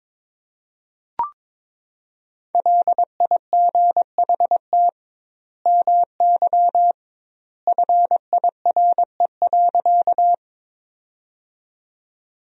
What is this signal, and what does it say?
Morse code, heard as a keyed tone, about 22 wpm, 700 Hz. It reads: LIGHT MY FIRE.
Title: 2x Word-Spacing - Sentences from Top 400 Words - 22wpm